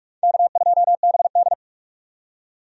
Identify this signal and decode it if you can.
Morse code, heard as a keyed tone, about 46 wpm, 700 Hz. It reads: K2BD
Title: USA Call Signs - 4 Characters - 45wpm